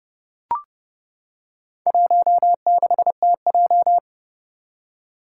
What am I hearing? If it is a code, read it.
Morse code, heard as a keyed tone, about 30 wpm, 700 Hz. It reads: 16TJ
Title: Four Challenging Letter-Number Combinations 30wpm